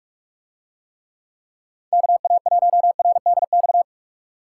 Morse code, heard as a keyed tone, about 44 wpm, 700 Hz. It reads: KA1RDX